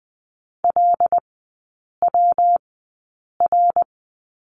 Morse code, heard as a keyed tone, about 20 wpm, spaced as for 8 wpm, 700 Hz. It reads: LWR
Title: 172 Lesson 28e - Number 9 - Three Characters 20wpm